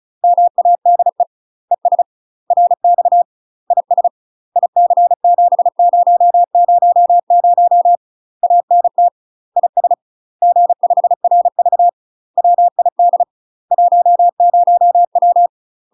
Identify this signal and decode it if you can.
Morse code, heard as a keyed tone, about 36 wpm, 700 Hz. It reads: MADE ES RX IS IC7000 ANT IS G5RV WID 10W
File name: UK MASTER QSOs TRAINING / 35WPM / UK MASTER QSOs 0029